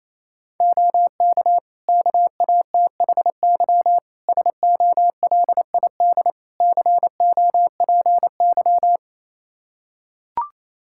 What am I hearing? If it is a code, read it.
Morse code, heard as a keyed tone, about 28 wpm, 700 Hz. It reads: OK KATHY SOLID COPY
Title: Practice QSO - Exchange 3 - More Info - 28wpm